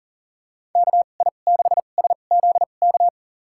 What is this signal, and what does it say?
Morse code, heard as a keyed tone, about 40 wpm, spaced as for 29 wpm, 700 Hz. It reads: KI6SZK
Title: USA Call Signs - 6 Characters - 2x Character Spacing - 40wpm